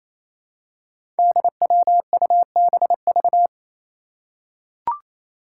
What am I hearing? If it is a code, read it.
Morse code, heard as a keyed tone, about 29 wpm, 700 Hz. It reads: DWUBV
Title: Five Challenging Letter-Number Combinations 28wpm